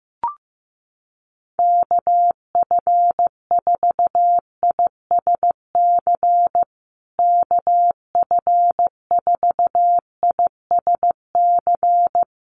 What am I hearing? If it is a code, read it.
Morse code, heard as a keyed tone, about 15 wpm, 700 Hz. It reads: KF4ISC KF4ISC
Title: USA Call Signs Repeated 3x 15wpm